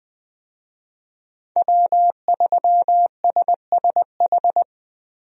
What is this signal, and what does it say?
Morse code, heard as a keyed tone, about 20 wpm, 700 Hz. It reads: W3SSH
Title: USA Call Signs - 5 Characters - Repeated 2x - 20wpm